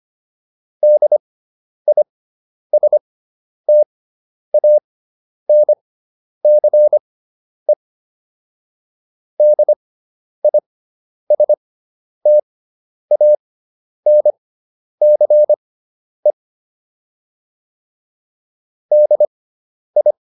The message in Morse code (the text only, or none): DISTANCE DISTANCE DI